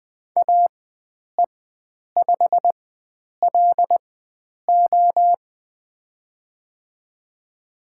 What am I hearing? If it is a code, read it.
Morse code, heard as a keyed tone, about 20 wpm, spaced as for 9 wpm, 700 Hz. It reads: AE5LO
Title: USA Call Signs - 5 Characters - 4x Character Spacing - 20wpm